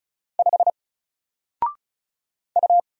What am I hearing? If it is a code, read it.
Morse code, heard as a keyed tone, about 36 wpm, 700 Hz. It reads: 5 U
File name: Single-Letter-Number-35wpm